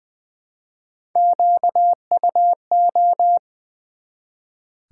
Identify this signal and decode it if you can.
Morse code, heard as a keyed tone, about 20 wpm, 700 Hz. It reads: QUO
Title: Build A Word - 1 Letter At A Time - 3 to 5-Letter Words - 20wpm